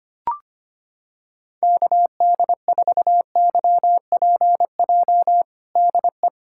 Morse code, heard as a keyed tone, about 25 wpm, 700 Hz. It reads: KD4YPJ DE
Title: USA Callsign DE Callsign - 6 Characters - 25wpm